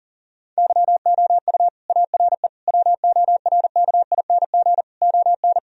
Morse code, heard as a keyed tone, about 40 wpm, 700 Hz. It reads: YOU ARE WORKING ON
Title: Sentences from Top 1000 Words - Review Part 4 - 40wpm